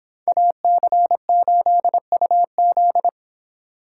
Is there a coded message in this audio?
Yes: Morse code from 0.3 to 3.1 s, heard as a keyed tone, about 29 wpm, 700 Hz. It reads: AC8UZ